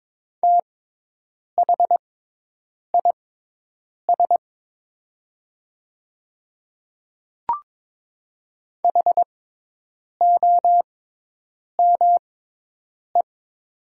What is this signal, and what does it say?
Morse code, heard as a keyed tone, about 22 wpm, spaced as for 8 wpm, 700 Hz. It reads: THIS HOME
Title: 6x Character Spacing - Top 200 Words - 22wpm